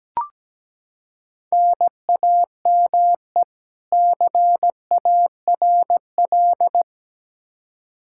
Morse code, heard as a keyed tone, about 17 wpm, 700 Hz. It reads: NAME CARL
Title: Practice QSO - Exchange 1 - Beginning - 17wpm